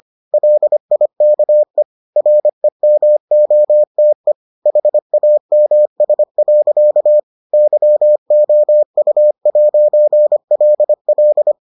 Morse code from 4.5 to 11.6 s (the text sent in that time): HAMS. YOU'LL